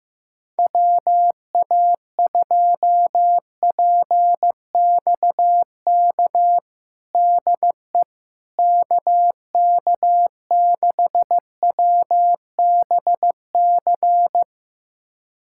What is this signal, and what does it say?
Morse code, heard as a keyed tone, about 15 wpm, 700 Hz. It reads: WA2PXK DE KK6WBC
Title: USA Callsign DE Callsign - 6 Characters - 15wpm